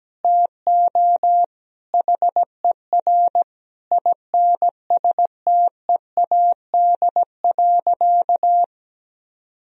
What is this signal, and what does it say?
Morse code, heard as a keyed tone, about 17 wpm, 700 Hz. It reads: TO HER INSTEAD.
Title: Sentences from Top 1000 Words - Review Part 4 - 17wpm